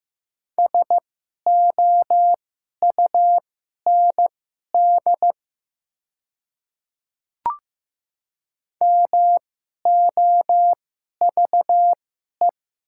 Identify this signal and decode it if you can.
Morse code, heard as a keyed tone, about 15 wpm, spaced as for 11 wpm, 690 Hz. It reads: SOUND MOVE
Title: 2x Character Spacing - Top 300 Words - 15wpm